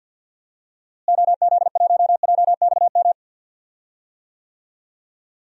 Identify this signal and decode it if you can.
Morse code, heard as a keyed tone, about 48 wpm, 700 Hz. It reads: OZ1JXM
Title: DXCC-100 Call Signs - 6 Characters - Repeated 2x - 50wpm